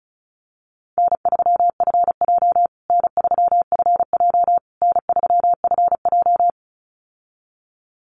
Morse code, heard as a keyed tone, about 34 wpm, 700 Hz. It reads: N3FJ N3FJ N3FJ
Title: USA Call Signs Repeated 3x 35wpm